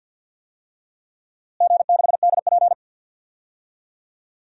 Morse code, heard as a keyed tone, about 53 wpm, 700 Hz. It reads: G6DP
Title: DXCC-100 Call Signs - 4 Characters - Repeated 2x - 50wpm